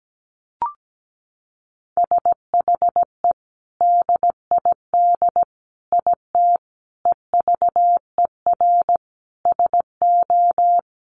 Morse code, heard as a keyed tone, about 17 wpm, 700 Hz. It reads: SHE DID IT EVER SO
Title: Sentences from Top 400 Words 17wpm